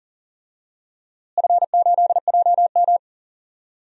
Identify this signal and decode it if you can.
Morse code, heard as a keyed tone, about 40 wpm, 700 Hz. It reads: F8JM
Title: DXCC-100 Call Signs - 4 Characters - Repeated 2x - 40wpm